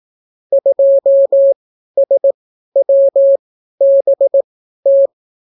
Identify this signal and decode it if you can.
Morse code, heard as a keyed tone, about 18 wpm, spaced as for 12 wpm, 550 Hz. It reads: 2SWBT